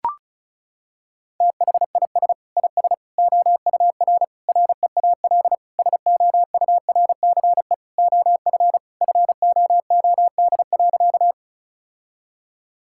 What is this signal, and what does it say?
Morse code, heard as a keyed tone, about 36 wpm, 700 Hz. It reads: THIS IS OUR REAL SOURCE OF FOOD.
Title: Sentences from Top 1000 Words - Review Part 5 - 35wpm